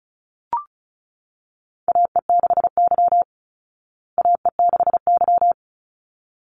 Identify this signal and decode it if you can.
Morse code, heard as a keyed tone, about 34 wpm, 700 Hz. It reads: AE6Y AE6Y
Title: Call Signs Repeated 3x - 35wpm